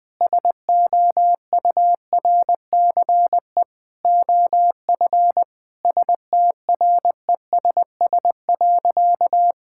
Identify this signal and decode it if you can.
Morse code, heard as a keyed tone, about 20 wpm, 700 Hz. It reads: SOURCE OF STRESS.
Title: Sentences from Top 1000 Words - Review Part 6 - 20wpm